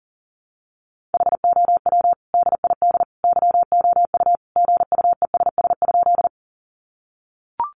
Morse code, heard as a keyed tone, about 40 wpm, 700 Hz. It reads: HOW DID YOU GUESS?
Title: Sentences from Top 1500 Words 40wpm